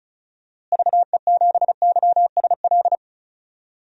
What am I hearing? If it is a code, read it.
Morse code, heard as a keyed tone, about 34 wpm, 700 Hz. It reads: VE7YSL
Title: DXCC-100 Call Signs - 6 Characters - Repeated 2x - 35wpm